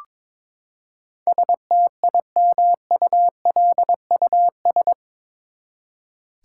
Morse code, heard as a keyed tone, about 22 wpm, 700 Hz. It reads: STIMULUS
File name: practice / 22wpm / Eight Letter Words - 22wpm